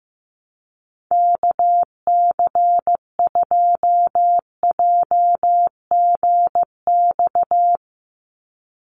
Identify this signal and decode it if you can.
Morse code, heard as a keyed tone, about 15 wpm, 700 Hz. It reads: KC2JGX
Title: USA Call Signs 15wpm